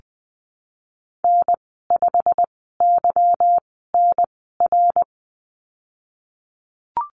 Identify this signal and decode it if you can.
Morse code, heard as a keyed tone, about 20 wpm, spaced as for 15 wpm, 700 Hz. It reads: N5YNR
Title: 154 Lesson 25g - Letter p - Call Signs 20wpm